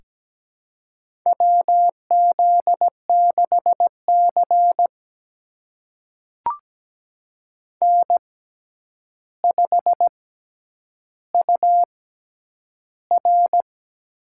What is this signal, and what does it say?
Morse code, heard as a keyed tone, about 17 wpm, spaced as for 9 wpm, 700 Hz. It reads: WZ6C N5UR
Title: USA Call Signs - 4 Characters - 6x Character Spacing - 17wpm